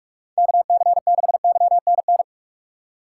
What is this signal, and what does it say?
Morse code, heard as a keyed tone, about 43 wpm, 700 Hz. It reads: KC6YNN
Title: USA Call Signs - 6 Characters - 6x Character Spacing - 45wpm